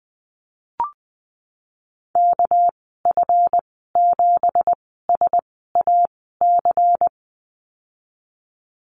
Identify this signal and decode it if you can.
Morse code, heard as a keyed tone, about 20 wpm, spaced as for 15 wpm, 700 Hz. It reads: KF7SAC
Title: 207 Lesson 33g - Letter j - Call Signs 20wpm